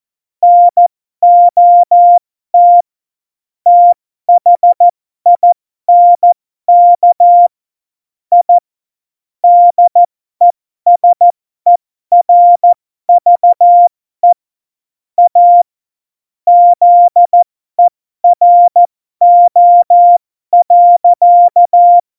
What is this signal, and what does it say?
Morse code, heard as a keyed tone, about 14 wpm, spaced as for 12 wpm, 700 Hz. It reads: NOT THINK I DESERVE A ZERO.
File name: Morsekurs / Witze